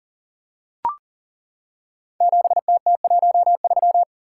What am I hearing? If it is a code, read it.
Morse code, heard as a keyed tone, about 42 wpm, 690 Hz. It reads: 7TT13
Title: Code Groups - 5 Letter:Numbers - 40wpm